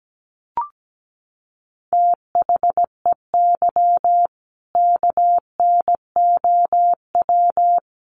Morse code, heard as a keyed tone, about 17 wpm, 700 Hz. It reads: THEY KNOW
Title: Sentences from Top 800 Words 17wpm